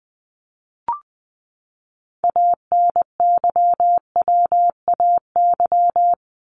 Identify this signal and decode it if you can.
Morse code, heard as a keyed tone, about 20 wpm, 700 Hz. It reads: ANYWAY
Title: Top 1501-1750 Words 20wpm